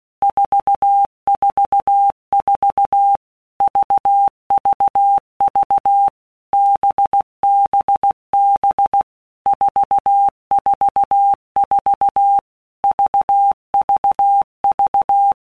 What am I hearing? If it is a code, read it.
Morse code, heard as a keyed tone, about 16 wpm, 800 Hz. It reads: 444 VVV BBB 444 VVV
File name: Basic Letter Groups By WPM's / Groups@15 / VB46@15